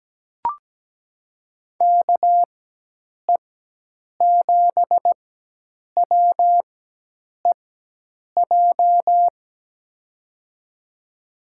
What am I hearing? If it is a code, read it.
Morse code, heard as a keyed tone, about 17 wpm, spaced as for 8 wpm, 700 Hz. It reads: KE7WEJ
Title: USA Call Signs - 6 Characters - 4x Character Spacing - 17wpm